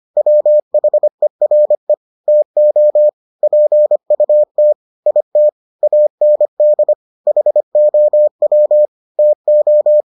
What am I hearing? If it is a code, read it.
Morse code, heard as a keyed tone, about 25 wpm, 600 Hz. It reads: WHERE TO PUT IT AND HOW TO